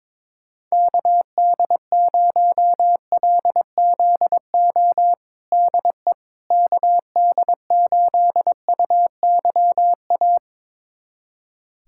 Morse code, heard as a keyed tone, about 22 wpm, 700 Hz. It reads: KD0LZO DE KD8UYA
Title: USA Callsign DE Callsign - 6 Characters - 22wpm